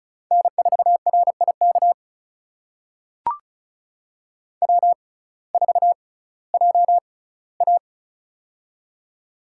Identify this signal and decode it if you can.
Morse code, heard as a keyed tone, about 36 wpm, spaced as for 19 wpm, 700 Hz. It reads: N4RIK W4JA
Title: USA Call Signs - 6x Character Spacing - 35wpm